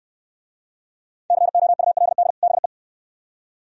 Morse code, heard as a keyed tone, about 69 wpm, 700 Hz. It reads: COULD BE